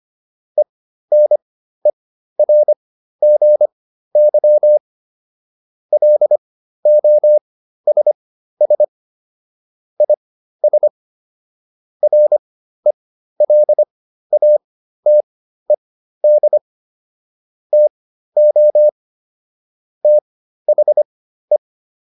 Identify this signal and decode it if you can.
Morse code, heard as a keyed tone, about 25 wpm, spaced as for 13 wpm, 600 Hz. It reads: ENERGY LOSS IS RELATED TO THE
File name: SS211 13